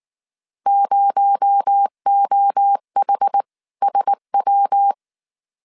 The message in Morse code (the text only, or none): 0OHSW